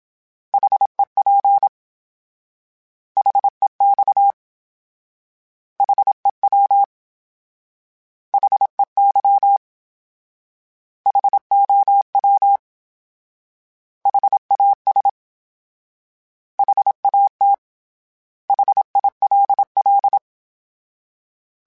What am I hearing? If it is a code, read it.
Morse code, heard as a keyed tone, about 27 wpm, 800 Hz. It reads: HEP HEX HEW HEY HOW HAS HAT HILL
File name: H Prefixes / H mp3s